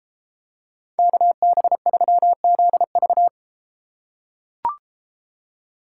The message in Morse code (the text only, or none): KB3ZV